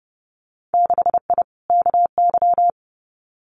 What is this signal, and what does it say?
Morse code, heard as a keyed tone, about 32 wpm, 700 Hz. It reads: 6I KY
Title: Winter Field Day Exchanges 30wpm